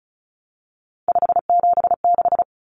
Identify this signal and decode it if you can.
Morse code, heard as a keyed tone, about 34 wpm, 700 Hz. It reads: 576
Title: Common Cut Numbers - 3-Digit 35wpm